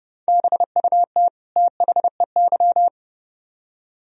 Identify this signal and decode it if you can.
Morse code, heard as a keyed tone, about 30 wpm, 700 Hz. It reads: BUT THEY